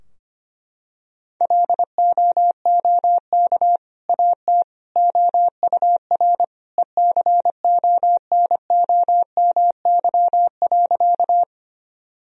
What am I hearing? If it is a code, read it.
Morse code, heard as a keyed tone, about 25 wpm, 700 Hz. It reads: LOOK AT OUR ECONOMY.
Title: Sentences from Top 1000 Words - Review Part 3 - 25wpm